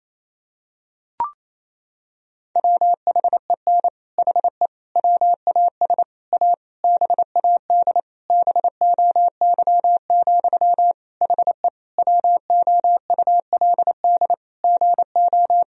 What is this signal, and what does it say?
Morse code, heard as a keyed tone, about 28 wpm, 700 Hz. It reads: WHEN HE WAS A BAD BOY, HE WOULD GO
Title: Sentences from Top 1000 Words - Review Part 3 - 28wpm